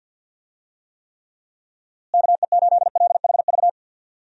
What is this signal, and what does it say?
Morse code, heard as a keyed tone, about 52 wpm, 700 Hz. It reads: KE8LHV
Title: USA Call Signs - 6 Characters - Repeated 3x - 50wpm